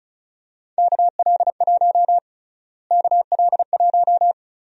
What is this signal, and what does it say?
Morse code, heard as a keyed tone, about 36 wpm, 700 Hz. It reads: KL1 KL1
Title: USA Call Sign Prefixes - Repeated 2x - 35wpm